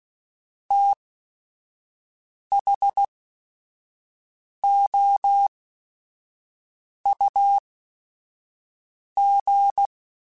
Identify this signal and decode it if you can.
Morse code, heard as a keyed tone, about 16 wpm, spaced as for 5 wpm, 800 Hz. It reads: THOUG
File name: Technical Subjects / Technical Subjects-mp3s / Sw Pwr Sup @5